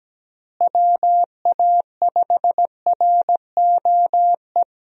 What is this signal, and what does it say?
Morse code, heard as a keyed tone, about 17 wpm, 700 Hz. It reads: WA5ROE
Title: USA Call Signs - 6 Characters - Repeated 3x - 17wpm